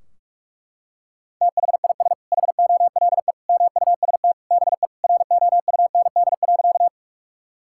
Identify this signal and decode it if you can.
Morse code, heard as a keyed tone, about 41 wpm, 700 Hz. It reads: THIS HOLE MUST BE ROUND.
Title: Sentences from Top 1600 Words 45wpm